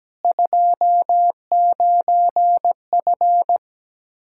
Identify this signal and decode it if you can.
Morse code, heard as a keyed tone, about 17 wpm, 700 Hz. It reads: 29F
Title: Practice QSO - Exchange 2 - Info - 17wpm